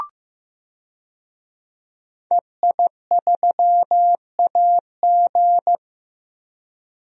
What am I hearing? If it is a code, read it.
Morse code, heard as a keyed tone, about 15 wpm, 700 Hz. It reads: EI3AG
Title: DXCC-100 Call Signs - 5 Characters - Repeated 3x - 15wpm